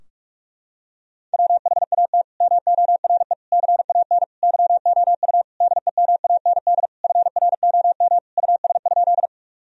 Morse code, heard as a keyed tone, about 48 wpm, 700 Hz. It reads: WHAT MORE CAN YOU DEMAND FROM US?
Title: Sentences from Top 1000 Words - Review Part 4 - 45wpm